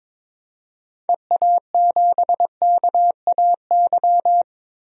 Morse code, heard as a keyed tone, about 22 wpm, 700 Hz. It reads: EA7KAY
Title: DXCC-100 Call Signs - 6 Characters - 22wpm